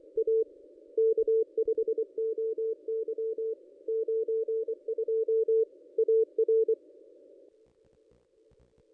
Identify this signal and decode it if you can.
Morse code, heard as a keyed tone, about 24 wpm, 440 Hz. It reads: A K5OY 92 AR
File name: contests / ss-cw-k5tr-op-n5rz -1400